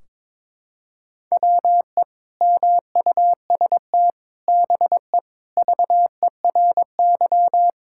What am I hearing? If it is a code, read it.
Morse code, heard as a keyed tone, about 22 wpm, 700 Hz. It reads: WE MUST BE VERY
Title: Sentences from Top 1600 Words 22wpm